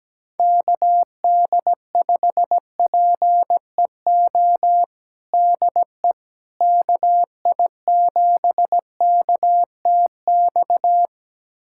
Morse code, heard as a keyed tone, about 17 wpm, 700 Hz. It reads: KD5PEO DE KI7KTX
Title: USA Callsign DE Callsign - 6 Characters - 17wpm